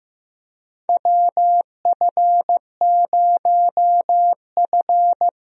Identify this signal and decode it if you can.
Morse code, heard as a keyed tone, about 15 wpm, 700 Hz. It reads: WF0F
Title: USA Call Signs - 4 Characters - Repeated 2x - 15wpm